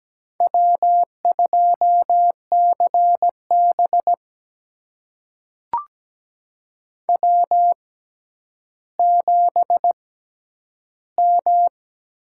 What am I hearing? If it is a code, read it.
Morse code, heard as a keyed tone, about 17 wpm, spaced as for 10 wpm, 700 Hz. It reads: W2CB W7M
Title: USA Call Signs - 6x Character Spacing - 17wpm